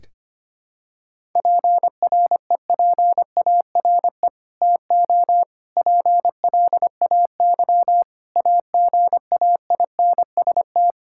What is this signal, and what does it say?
Morse code, heard as a keyed tone, about 25 wpm, 700 Hz. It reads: PREPARE TO PLAY AGAINST
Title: Sentences from Top 1000 Words - Review Part 6 - 25wpm